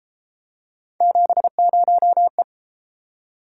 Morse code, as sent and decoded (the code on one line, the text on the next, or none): --... ----- .
70E